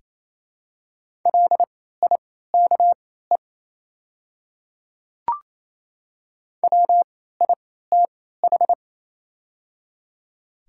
Morse code, heard as a keyed tone, about 28 wpm, spaced as for 16 wpm, 700 Hz. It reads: LIKE WITH
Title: 3x Character Spacing - Top 200 Words - 28wpm